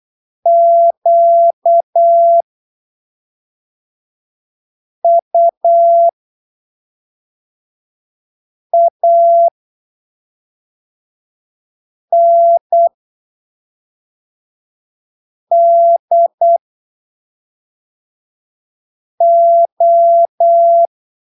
Morse code, heard as a keyed tone, about 8 wpm, spaced as for 3 wpm, 680 Hz. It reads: QUANDO
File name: arquivos / texto / texto 21 08ppm